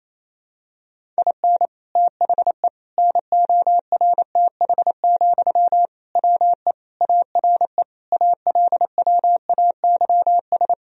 Morse code, heard as a keyed tone, about 28 wpm, 700 Hz. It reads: IN THE NORTH, WE ARE ALWAYS
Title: Sentences from Top 1000 Words - Review Part 1 - 28wpm